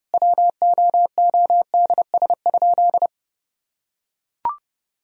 Morse code, heard as a keyed tone, about 30 wpm, 700 Hz. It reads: WOODS?
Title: Sentences from Top 1000 Words - Review Part 2 - 30wpm